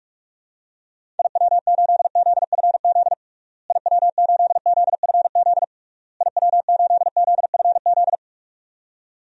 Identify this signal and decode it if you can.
Morse code, heard as a keyed tone, about 41 wpm, 700 Hz. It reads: IW8ZFZ IW8ZFZ IW8ZFZ
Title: DXCC-100 Call Signs - 6 Characters - Repeated 3x - 45wpm